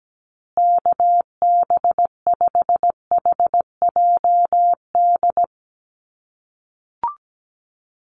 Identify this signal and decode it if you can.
Morse code, heard as a keyed tone, about 17 wpm, 700 Hz. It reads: KB5HJD
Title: USA Call Signs Repeated 2x 17wpm